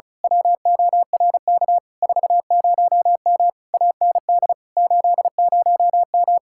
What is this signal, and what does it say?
Morse code, heard as a keyed tone, about 32 wpm, 700 Hz. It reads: WORK 40M AND 80M
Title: Speed-Racing Common QSO Phrases - 35wpm